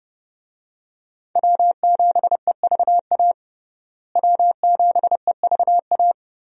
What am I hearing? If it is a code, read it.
Morse code, heard as a keyed tone, about 32 wpm, 700 Hz. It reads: W7EVA W7EVA
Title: USA Call Signs - 5 Characters - Repeated 3x - 30wpm